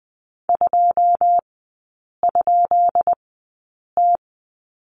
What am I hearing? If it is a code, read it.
Morse code, heard as a keyed tone, about 20 wpm, spaced as for 8 wpm, 700 Hz. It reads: 2?T
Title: 218 Lesson 35e - Number 0 - Three Characters 20wpm